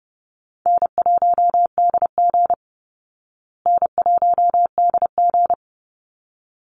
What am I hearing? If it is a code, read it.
Morse code, heard as a keyed tone, about 30 wpm, 700 Hz. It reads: N1DG N1DG
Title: Easy Call Signs Repeated 2x - 30wpm